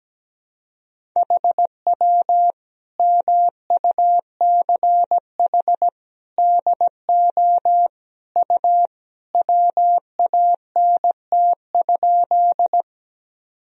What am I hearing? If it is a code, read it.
Morse code, heard as a keyed tone, about 17 wpm, 700 Hz. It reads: HW MUCH DO U WANT?
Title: Sentences from Top 200 Words and Abbreviations 17wpm